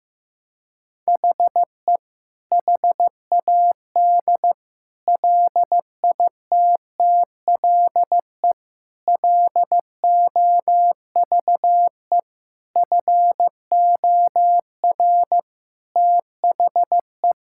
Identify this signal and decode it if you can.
Morse code, heard as a keyed tone, about 15 wpm, 700 Hz. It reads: HE HAD LITTLE LOVE FOR THE
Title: Sentences from Top 1600 Words 15wpm